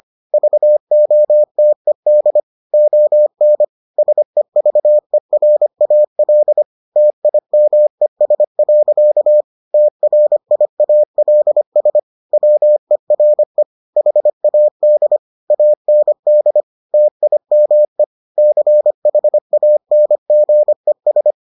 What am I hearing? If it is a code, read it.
Morse code, heard as a keyed tone, about 25 wpm, 600 Hz. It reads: VOTED ON SEVERAL TIMES. TRIALS WERE HAD AND TIME CHANGES